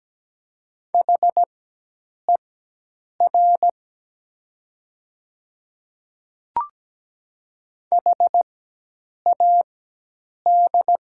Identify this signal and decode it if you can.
Morse code, heard as a keyed tone, about 17 wpm, spaced as for 8 wpm, 700 Hz. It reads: HER HAD
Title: 4x Character Spacing - Top 100 Words - 17wpm